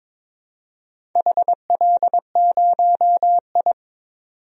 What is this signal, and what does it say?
Morse code, heard as a keyed tone, about 22 wpm, 700 Hz. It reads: HL0I
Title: DXCC-100 Call Signs - 4 Characters - 22wpm